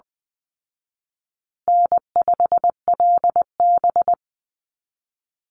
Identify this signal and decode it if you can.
Morse code, heard as a keyed tone, about 20 wpm, 700 Hz. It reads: N5LB
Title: Easy Call Signs Repeated 3x - 20wpm